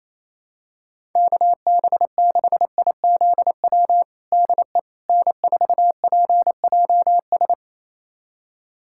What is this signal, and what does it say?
Morse code, heard as a keyed tone, about 27 wpm, 700 Hz. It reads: KB6IZW DE N4PJS
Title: USA Callsign DE Callsign 28wpm